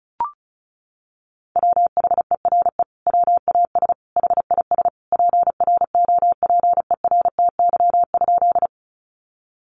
Morse code, heard as a keyed tone, about 34 wpm, 700 Hz. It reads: WHERE WAS HIS PROPERTY?
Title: Sentences from Top 1000 Words 35wpm